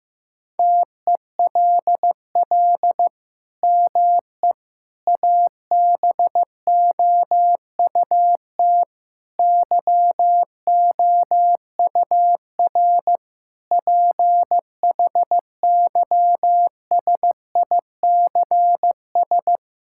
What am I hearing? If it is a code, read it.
Morse code, heard as a keyed tone, about 16 wpm, 700 Hz. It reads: TELL ME ABOUT YOUR PHYSICS
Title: Sentences from Top 1000 Words - Review Part 5 - 15wpm